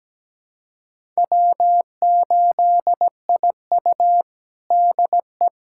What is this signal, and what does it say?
Morse code, heard as a keyed tone, about 17 wpm, 700 Hz. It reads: W8IU DE
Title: USA Callsign DE Callsign - 4 Characters - 17wpm